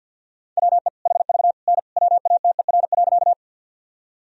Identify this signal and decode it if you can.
Morse code, heard as a keyed tone, about 47 wpm, 700 Hz. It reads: WE HV N WATER.
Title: Sentences from Top 100 Words and Abbreviations 50wpm